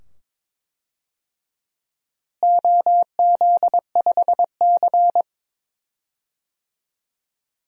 Morse code, heard as a keyed tone, about 22 wpm, 700 Hz. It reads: OZ5C
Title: DXCC-100 Call Signs - 4 Characters - Repeated 2x - 22wpm